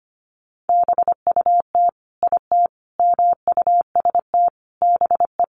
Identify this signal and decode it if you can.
Morse code, heard as a keyed tone, about 25 wpm, 700 Hz. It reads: BUT IT MUST BE
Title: Sentences from Top 600 Words 25wpm